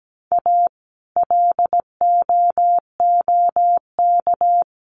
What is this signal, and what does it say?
Morse code, heard as a keyed tone, about 17 wpm, 700 Hz. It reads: A LOOK
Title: Sets of 4 Words 17wpm